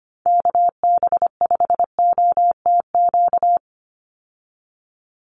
Morse code, heard as a keyed tone, about 25 wpm, 700 Hz. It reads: KB5OTQ
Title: USA Call Signs 25wpm